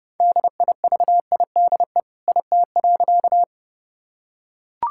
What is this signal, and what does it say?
Morse code, heard as a keyed tone, about 32 wpm, 700 Hz. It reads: DIVIDE IT.
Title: Sentences from Top 1600 Words 30wpm